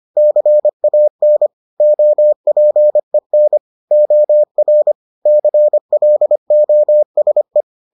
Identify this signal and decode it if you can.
Morse code, heard as a keyed tone, about 25 wpm, 600 Hz. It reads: CAN OPEN OR CLOSE